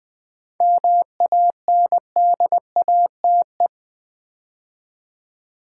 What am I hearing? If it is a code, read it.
Morse code, heard as a keyed tone, about 20 wpm, 700 Hz. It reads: MANDATE
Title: 7 Letter Words - First 3 Letters Given - 20wpm